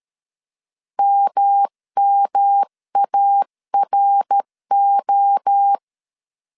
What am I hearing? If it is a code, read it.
Morse code, heard as a keyed tone, about 13 wpm, 800 Hz. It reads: MMARO
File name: CW / DET / 1 3 1500C S A 12wpm 2905